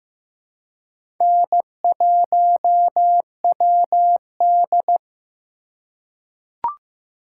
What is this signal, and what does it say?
Morse code, heard as a keyed tone, about 15 wpm, 700 Hz. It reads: N1WD
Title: USA Call Signs - 4 Characters - 3x Character Spacing - 15wpm